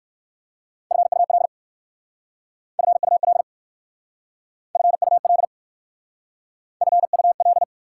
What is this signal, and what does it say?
Morse code, heard as a keyed tone, about 62 wpm, 700 Hz. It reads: FUL FUL FUL FUL